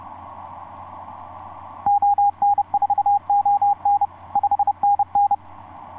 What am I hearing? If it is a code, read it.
Morse code, heard as a keyed tone, about 30 wpm, 810 Hz, with radio noise. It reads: ON4ON 5NN
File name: s79gm on4on etc 15m cw 30mar2010 0653